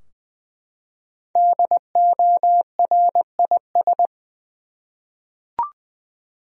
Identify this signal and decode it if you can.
Morse code, heard as a keyed tone, about 20 wpm, 700 Hz. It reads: DORIS
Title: USA Common Names - Max 5 Letters - 20wpm